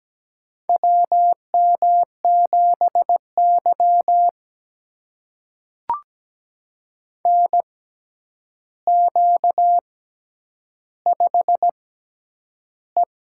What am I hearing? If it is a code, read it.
Morse code, heard as a keyed tone, about 17 wpm, spaced as for 9 wpm, 700 Hz. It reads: WM7Y NQ5E
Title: USA Call Signs - 4 Characters - 6x Character Spacing - 17wpm